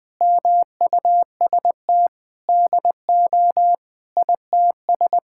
Morse code, heard as a keyed tone, about 21 wpm, 700 Hz. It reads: MUST DO ITS
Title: Sentences from Top 1000 Words - Review Part 5 - 20wpm